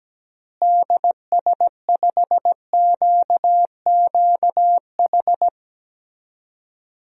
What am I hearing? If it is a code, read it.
Morse code, heard as a keyed tone, about 17 wpm, 700 Hz. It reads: DS5QQH
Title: DXCC-100 Call Signs - 6 Characters - Repeated 3x - 17wpm